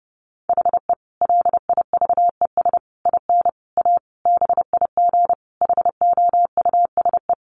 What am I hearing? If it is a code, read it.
Morse code, heard as a keyed tone, about 30 wpm, 700 Hz. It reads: HE LIVES IN A BIG HOUSE
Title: Additional Easy Phrases 30wpm